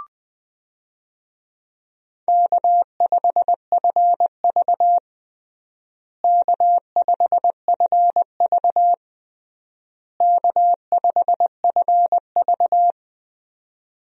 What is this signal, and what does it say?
Morse code, heard as a keyed tone, about 20 wpm, 700 Hz. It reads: K5FV K5FV K5FV
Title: USA Call Signs - 4 Characters - Repeated 3x - 20wpm